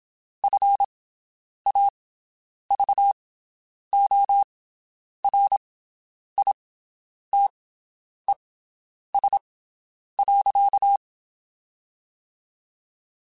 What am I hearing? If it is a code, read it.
Morse code, heard as a keyed tone, about 27 wpm, spaced as for 9 wpm, 800 Hz. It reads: FAVORITES.
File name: QSOs Spaced